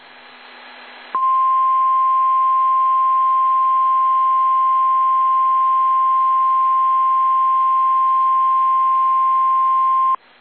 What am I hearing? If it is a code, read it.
Emergency Alert System (SAME) tones (the 1050 Hz attention tone). It is heard from 1.1 to 10.2 s.
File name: eas / KSDZKDJL / KSDZKDJL